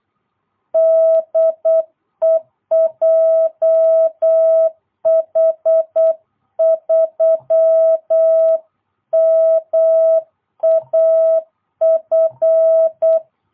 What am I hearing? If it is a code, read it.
Morse code, heard as a keyed tone, about 8 wpm, 660 Hz. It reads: DE JH3MAF